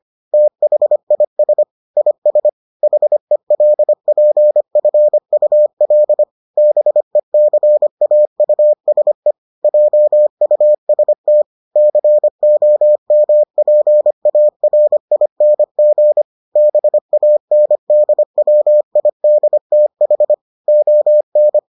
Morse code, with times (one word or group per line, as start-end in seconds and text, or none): 0.3-1.6 THIS
2.0-2.5 IS
2.8-6.2 HELPFUL
6.6-9.3 BECAUSE
9.6-11.4 JUST
11.8-16.2 COMPARING
16.6-20.3 BANDWIDTH
20.7-21.6 ON